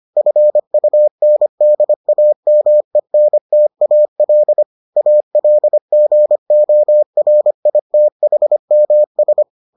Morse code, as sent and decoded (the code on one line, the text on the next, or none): ..-. ..- -. -.. .- -- . -. - .- .-.. / .- .-.. --. --- .-. .. - .... -- ...
FUNDAMENTAL ALGORITHMS